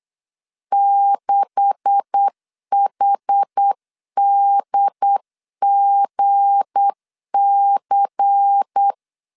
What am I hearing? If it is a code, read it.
Morse code, heard as a keyed tone, about 8 wpm, 800 Hz. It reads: 6HDGC
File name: LEZIONI / 9 286 8wpm 3425